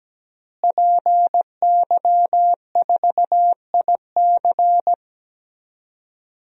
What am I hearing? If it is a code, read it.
Morse code, heard as a keyed tone, about 17 wpm, 700 Hz. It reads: PY4IC
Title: DXCC-100 Call Signs - 5 Characters - Repeated 2x - 17wpm